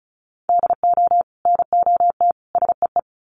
Morse code, heard as a keyed tone, about 34 wpm, 700 Hz. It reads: DO NOT SEE
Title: Sets of 3 Words 35wpm